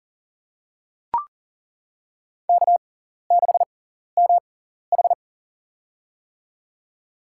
Morse code, heard as a keyed tone, about 40 wpm, spaced as for 14 wpm, 700 Hz. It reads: K6MH
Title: USA Call Signs - 4 Characters - 6x Character Spacing - 40wpm